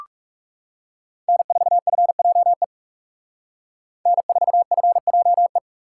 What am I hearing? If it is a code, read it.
Morse code, heard as a keyed tone, about 39 wpm, 700 Hz. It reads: N4FJE N4FJE